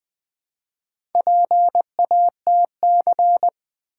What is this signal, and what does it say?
Morse code, heard as a keyed tone, about 20 wpm, 700 Hz. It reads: PATC
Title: Build A Word - 1 Letter At A Time - 3 to 5-Letter Words - 20wpm